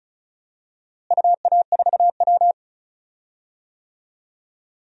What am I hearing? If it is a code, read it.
Morse code, heard as a keyed tone, about 36 wpm, 700 Hz. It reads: UA4W
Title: DXCC-100 Call Signs - 4 Characters - Repeated 2x - 35wpm